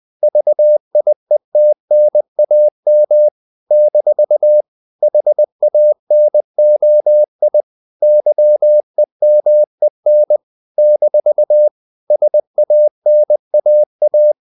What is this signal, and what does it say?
Morse code, heard as a keyed tone, about 20 wpm, 600 Hz. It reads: VIETNAM - HANOI YEMEN - SANAA
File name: CW / Practice / NationalCapitals20